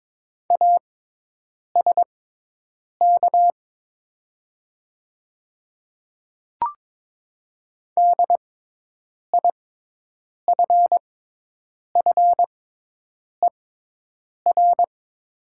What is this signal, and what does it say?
Morse code, heard as a keyed tone, about 22 wpm, spaced as for 8 wpm, 700 Hz. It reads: ASK DIFFER
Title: 6x Character Spacing - Top 200 Words - 22wpm